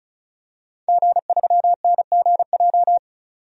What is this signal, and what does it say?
Morse code, heard as a keyed tone, about 36 wpm, 700 Hz. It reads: G3NGJ
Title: DXCC-100 Call Signs - 5 Characters - 35wpm